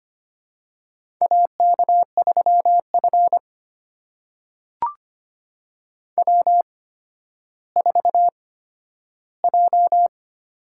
Morse code, heard as a keyed tone, about 25 wpm, spaced as for 12 wpm, 700 Hz. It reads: AK3F W4J
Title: USA Call Signs - 4 Characters - 8x Character Spacing - 25wpm